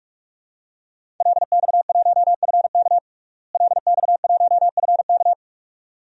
Morse code, heard as a keyed tone, about 42 wpm, 700 Hz. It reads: LX1FK LX1FK
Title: DXCC-100 Call Signs - 5 Characters - Repeated 2x - 45wpm